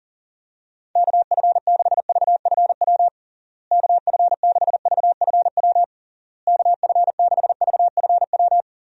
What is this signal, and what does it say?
Morse code, heard as a keyed tone, about 40 wpm, 700 Hz. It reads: KF6VFW KF6VFW KF6VFW
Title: USA Call Signs - 6 Characters - Repeated 3x - 40wpm